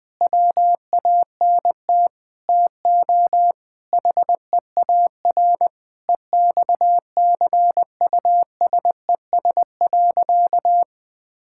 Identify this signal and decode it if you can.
Morse code, heard as a keyed tone, about 20 wpm, 700 Hz. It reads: WANT TO HEAR EXCUSES.
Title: Sentences with 5-Word Phrases - 20wpm